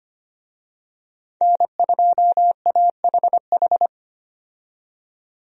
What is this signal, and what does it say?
Morse code, heard as a keyed tone, about 25 wpm, 700 Hz. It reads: N2AHH
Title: USA Call Signs - 5 Characters - 25wpm